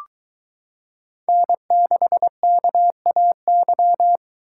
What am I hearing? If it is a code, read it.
Morse code, heard as a keyed tone, about 22 wpm, 700 Hz. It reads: N6KAY